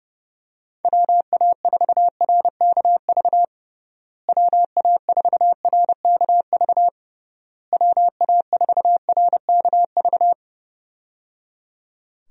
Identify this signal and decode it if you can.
Morse code, heard as a keyed tone, about 30 wpm, 700 Hz. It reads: WA4RKV WA4RKV WA4RKV
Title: USA Call Signs - 6 Characters - Repeated 3x - 30wpm